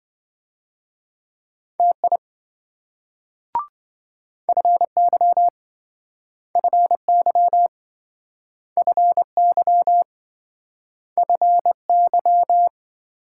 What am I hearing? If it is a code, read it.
Morse code, heard as a keyed tone, about 27 wpm, 700 Hz. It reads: TI FY FY FY FY